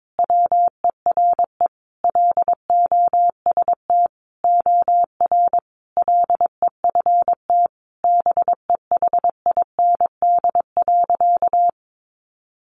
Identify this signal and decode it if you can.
Morse code, heard as a keyed tone, about 22 wpm, 700 Hz. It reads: WERE LOST OR LEFT BEHIND.
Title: Sentences from Top 1400 Words 22wpm